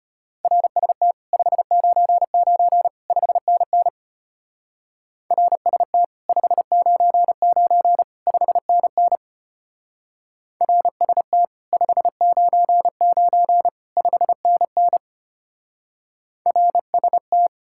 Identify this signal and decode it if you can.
Morse code, heard as a keyed tone, about 31 wpm, 700 Hz. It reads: RST 599 5NN RST 599 5NN RST 599 5NN RST
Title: Speed-Racing Common QSO Phrases - 25wpm